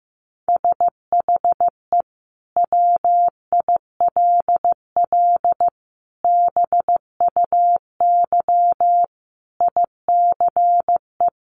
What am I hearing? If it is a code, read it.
Morse code, heard as a keyed tone, about 15 wpm, 700 Hz. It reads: SHE WILL BUY ICE
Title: Sentences from Top 1500 Words 15wpm